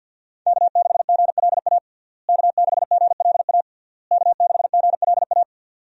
Morse code, heard as a keyed tone, about 50 wpm, 700 Hz. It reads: K6GLA K6GLA K6GLA
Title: USA Call Signs - 5 Characters - Repeated 3x - 50wpm